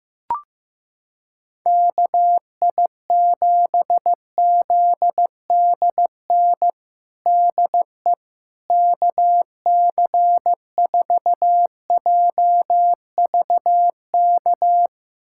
Morse code, heard as a keyed tone, about 15 wpm, 700 Hz. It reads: KI7ZDN DE KC4JVK
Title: USA Callsign DE Callsign - 6 Characters - 15wpm